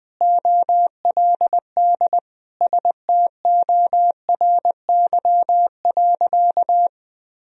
Morse code, heard as a keyed tone, about 20 wpm, 700 Hz. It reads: OLD STORY.
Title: Sentences from Top 1000 Words - Review Part 1 - 20wpm